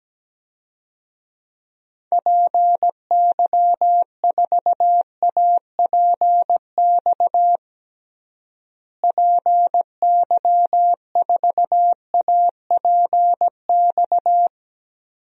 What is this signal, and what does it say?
Morse code, heard as a keyed tone, about 17 wpm, 700 Hz. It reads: PY4APX PY4APX
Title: DXCC-100 Call Signs - 6 Characters - Repeated 3x - 17wpm